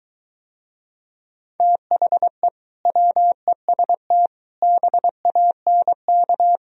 Morse code, heard as a keyed tone, about 23 wpm, 700 Hz. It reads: THE WEST BANK